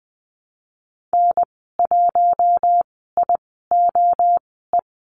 Morse code, heard as a keyed tone, about 20 wpm, spaced as for 14 wpm, 700 Hz. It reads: N1IOE
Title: 042 Lesson 8g - Number 1 - Call Signs 20wpm